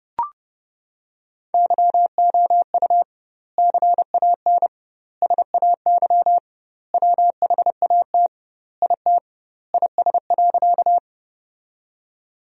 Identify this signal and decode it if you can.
Morse code, heard as a keyed tone, about 32 wpm, 700 Hz. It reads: YOU CAN SAY WHAT IT IS.
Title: 2x Word-Spacing - Sentences from Top 200 Words - 30wpm